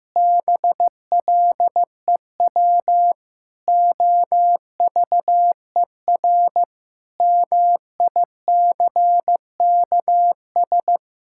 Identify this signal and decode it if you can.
Morse code, heard as a keyed tone, about 15 wpm, 700 Hz. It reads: BLEW OVER MICKS
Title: Pangram Challenge v2 - 15wpm